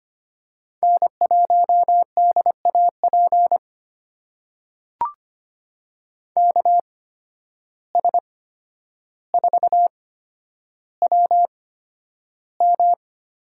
Morse code, heard as a keyed tone, about 25 wpm, spaced as for 11 wpm, 700 Hz. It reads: N1DAP KS4WM
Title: USA Call Signs - 5 Characters - 8x Character Spacing - 25wpm